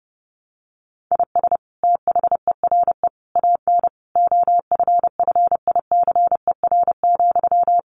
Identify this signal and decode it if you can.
Morse code, heard as a keyed tone, about 30 wpm, 700 Hz. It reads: IS THERE AN OFFICER,
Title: Sentences from Top 1200 Words 30wpm